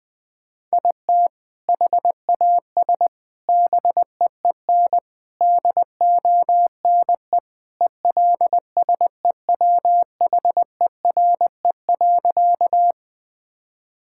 Morse code, heard as a keyed tone, about 21 wpm, 700 Hz. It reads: IT HAS BEEN DONE ELSEWHERE.
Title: Sentences from Top 1600 Words 20wpm